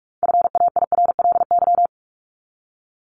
Morse code, heard as a keyed tone, about 44 wpm, 700 Hz. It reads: FAIRLY
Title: LY Suffix - Top 100 Words with Base Words 45wpm